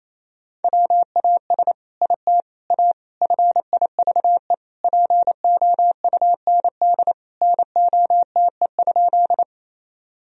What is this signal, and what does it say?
Morse code, heard as a keyed tone, about 28 wpm, 700 Hz. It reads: WAS IT A FIVE POUND NOTE?
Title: Sentences from Top 1000 Words - Review Part 2 - 28wpm